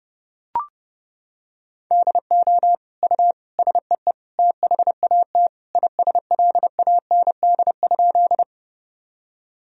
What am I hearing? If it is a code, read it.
Morse code, heard as a keyed tone, about 30 wpm, 700 Hz. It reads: DO U SEE THAT ISLAND?
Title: Sentences from Top 500 Words and Abbreviations 30wpm